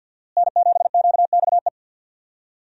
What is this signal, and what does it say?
Morse code, heard as a keyed tone, about 51 wpm, 700 Hz. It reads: N7QXE
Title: USA Call Signs - 5 Characters - Repeated 2x - 50wpm